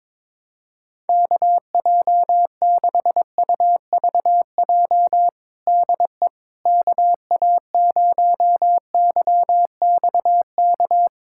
Morse code, heard as a keyed tone, about 22 wpm, 700 Hz. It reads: KJ6UVJ DE KA0YXK
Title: USA Callsign DE Callsign - 6 Characters - 22wpm